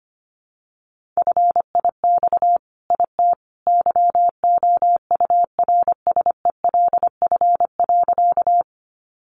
Morse code, heard as a keyed tone, about 25 wpm, 700 Hz. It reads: FIX IT YOURSELF.
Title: Sentences from Top 1400 Words 25wpm